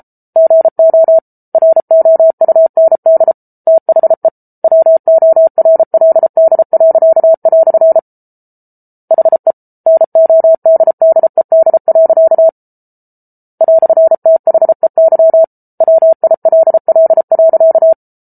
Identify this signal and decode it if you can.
Morse code, heard as a keyed tone, about 34 wpm, 650 Hz. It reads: GO ROUND THE WORLD." HE NODDED. "THEY WILL.